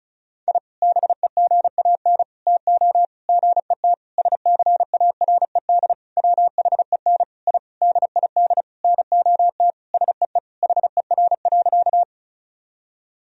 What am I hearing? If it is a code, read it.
Morse code, heard as a keyed tone, about 36 wpm, 700 Hz. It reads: I BEGAN TO GET SCARED WHEN I DID NOT SEE HER.
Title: Sentences from Top 1000 Words - Review Part 4 - 35wpm